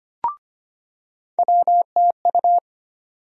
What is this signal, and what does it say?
Morse code, heard as a keyed tone, about 25 wpm, 700 Hz. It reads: WTU
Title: USA Call Sign Postfixes - 3 Letters 25wpm